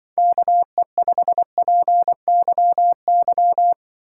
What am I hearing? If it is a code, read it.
Morse code, heard as a keyed tone, about 24 wpm, 700 Hz. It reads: KE5PYY
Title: Speed Racing - USA 2x3 Call Signs 20wpm